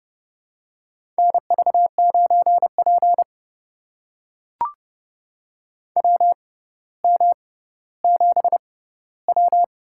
Morse code, heard as a keyed tone, about 30 wpm, spaced as for 15 wpm, 700 Hz. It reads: NV9P WM7W
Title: USA Call Signs - 4 Characters - 6x Character Spacing - 30wpm